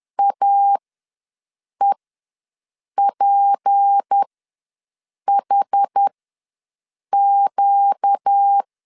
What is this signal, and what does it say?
Morse code, heard as a keyed tone, about 10 wpm, spaced as for 6 wpm, 800 Hz. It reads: AEPHQ
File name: LEZIONI / 6 PKX0 / 6 P ws 10wpm 3751